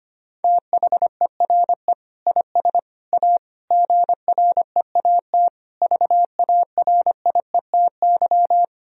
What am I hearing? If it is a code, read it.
Morse code, heard as a keyed tone, about 25 wpm, 700 Hz. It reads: THERE IS A GREAT VARIETY
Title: Sentences from Top 1000 Words - Review Part 4 - 25wpm